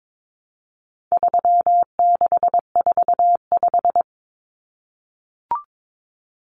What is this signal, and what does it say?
Morse code, heard as a keyed tone, about 22 wpm, 700 Hz. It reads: 3645
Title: Common Cut Numbers - 4-Digit 22wpm